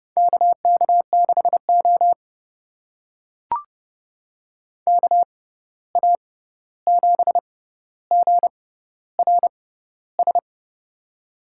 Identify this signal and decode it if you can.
Morse code, heard as a keyed tone, about 30 wpm, spaced as for 14 wpm, 700 Hz. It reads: KK6O KA7GRS
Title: USA Call Signs - 6x Character Spacing - 30wpm